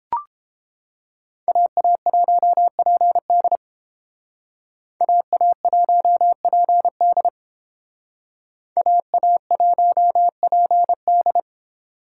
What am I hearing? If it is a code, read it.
Morse code, heard as a keyed tone, about 29 wpm, 700 Hz. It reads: AA1PD AA1PD AA1PD